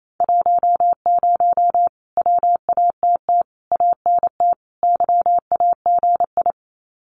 Morse code, heard as a keyed tone, about 28 wpm, 700 Hz. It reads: 10 WATT ANT YAGI
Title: Common QSO Phrases Encore 28wpm